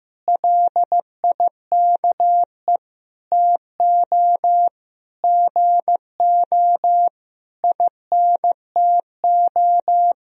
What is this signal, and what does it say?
Morse code, heard as a keyed tone, about 15 wpm, 700 Hz. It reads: LIKE TO GO INTO
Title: Sentences from Top 1000 Words - Review Part 1 - 15wpm